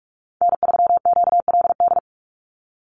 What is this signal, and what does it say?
Morse code, heard as a keyed tone, about 44 wpm, 700 Hz. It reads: N3QLD
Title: USA Call Signs 45wpm